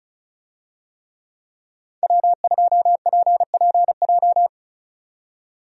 Morse code, heard as a keyed tone, about 36 wpm, 700 Hz. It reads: W2PPJ
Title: USA Call Signs - 5 Characters - 35wpm